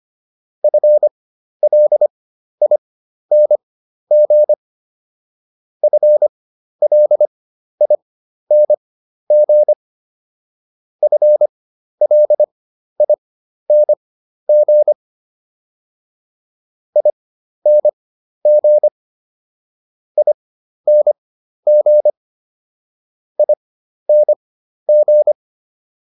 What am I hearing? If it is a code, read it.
Morse code, heard as a keyed tone, about 25 wpm, spaced as for 12 wpm, 600 Hz. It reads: FLING FLING FLING ING ING ING